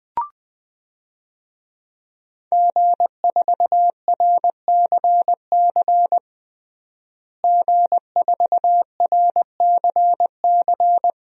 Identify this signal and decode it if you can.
Morse code, heard as a keyed tone, about 21 wpm, 700 Hz. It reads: G4RCC G4RCC
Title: DXCC-100 Call Signs - 5 Characters - Repeated 2x - 20wpm